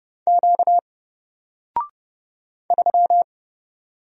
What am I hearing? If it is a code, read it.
Morse code, heard as a keyed tone, about 29 wpm, 700 Hz. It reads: Q 3
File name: Single-Letter-Number-30wpm